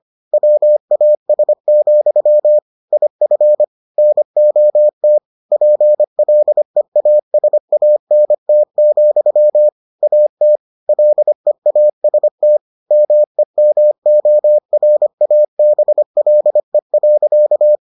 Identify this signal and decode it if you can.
Morse code, heard as a keyed tone, about 25 wpm, 600 Hz. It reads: WAS, IF NOT PLEASANT, AT LEAST MEMORABLE.